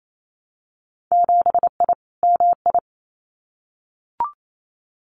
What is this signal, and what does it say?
Morse code, heard as a keyed tone, about 28 wpm, 700 Hz. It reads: 7I MI
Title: Winter Field Day Exchanges Encore 28wpm